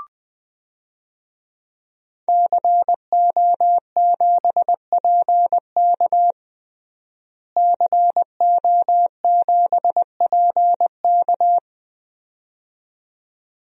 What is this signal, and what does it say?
Morse code, heard as a keyed tone, about 21 wpm, 700 Hz. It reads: CO7PK CO7PK
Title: DXCC-100 Call Signs - 5 Characters - Repeated 2x - 20wpm